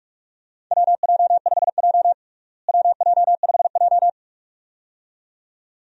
Morse code, heard as a keyed tone, about 41 wpm, 700 Hz. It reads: WJ5J WJ5J
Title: USA Call Signs - 4 Characters - Repeated 2x - 45wpm